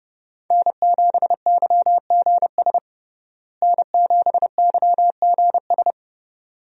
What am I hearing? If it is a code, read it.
Morse code, heard as a keyed tone, about 30 wpm, 700 Hz. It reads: N7YGS N7YGS
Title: USA Call Signs - 5 Characters - Repeated 2x - 30wpm